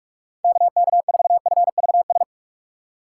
Morse code, heard as a keyed tone, about 46 wpm, 700 Hz. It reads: KK4FVS
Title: USA Call Signs - 6 Characters - 2x Character Spacing - 45wpm